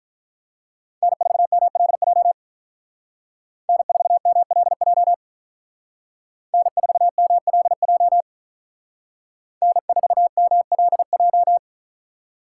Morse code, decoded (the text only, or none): N4MLJ N4MLJ N4MLJ N4MLJ